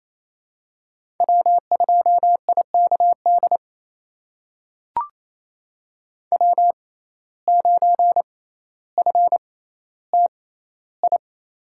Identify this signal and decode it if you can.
Morse code, heard as a keyed tone, about 28 wpm, spaced as for 14 wpm, 700 Hz. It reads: W2IKD W9FTI
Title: USA Call Signs - 5 Characters - 6x Character Spacing - 28wpm